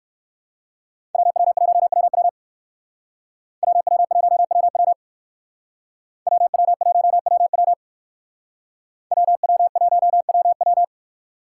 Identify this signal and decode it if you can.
Morse code, heard as a keyed tone, about 58 wpm, 700 Hz. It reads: WW1WW WW1WW WW1WW WW1WW